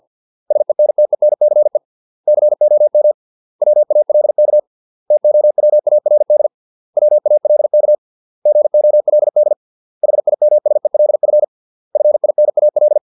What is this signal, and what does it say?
Morse code, heard as a keyed tone, about 50 wpm, 600 Hz. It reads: SENTENCE COM WALK TOWARD WALK GOLD HIMSELF FINAL